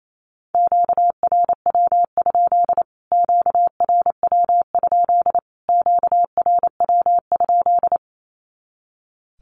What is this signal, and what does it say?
Morse code, heard as a keyed tone, about 28 wpm, 700 Hz. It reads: QRW? QRW? QRW?
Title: Q-Codes Repeated 3x 28wpm